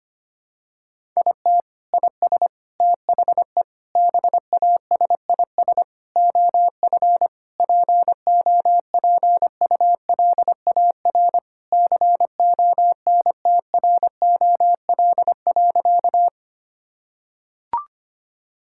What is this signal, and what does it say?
Morse code, heard as a keyed tone, about 25 wpm, 700 Hz. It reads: IT IS THE BASIS OF POPULAR CONTROL.
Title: Sentences from Top 1000 Words - Review Part 5 - 25wpm